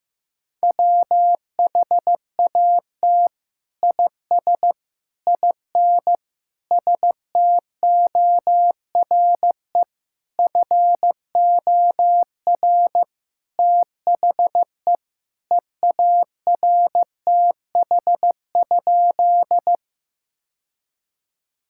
Morse code, heard as a keyed tone, about 15 wpm, 700 Hz. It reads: WHAT IS IN STORE FOR THE EARTH?
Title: Sentences from Top 1000 Words - Review Part 5 - 15wpm